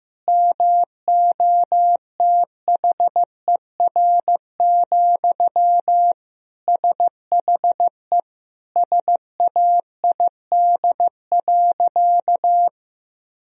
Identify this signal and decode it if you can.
Morse code, heard as a keyed tone, about 15 wpm, 700 Hz. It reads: MOTHER, SHE SAID.
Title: Sentences from Top 1000 Words - Review Part 1 - 15wpm